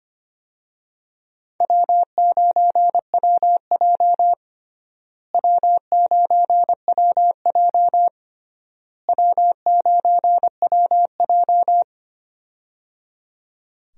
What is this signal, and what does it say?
Morse code, heard as a keyed tone, about 25 wpm, 700 Hz. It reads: W9WJ W9WJ W9WJ
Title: USA Call Signs - 4 Characters - Repeated 3x - 25wpm